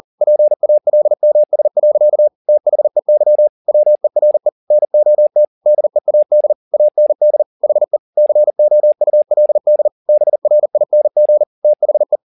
Morse code, heard as a keyed tone, about 40 wpm, 600 Hz. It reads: PALMS. THEY WERE NOT DEAD AND HE COULD BRING THE
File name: the old man and the sea 2 40wpm